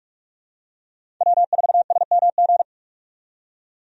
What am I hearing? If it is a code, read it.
Morse code, heard as a keyed tone, about 46 wpm, 700 Hz. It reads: W4SMG
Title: USA Call Signs - 5 Characters - 45wpm